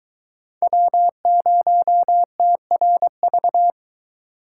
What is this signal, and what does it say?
Morse code, heard as a keyed tone, about 22 wpm, 700 Hz. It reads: W0TRV